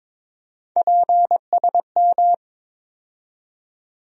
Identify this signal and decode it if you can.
Morse code, heard as a keyed tone, about 22 wpm, 700 Hz. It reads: PSM
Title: USA Call Sign Postfixes - 3 Letters 22wpm